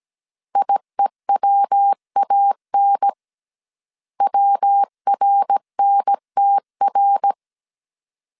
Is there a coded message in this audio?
Yes: Morse code from 0.6 to 7.3 s, heard as a keyed tone, about 17 wpm, 800 Hz. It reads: IEWAN WRNTR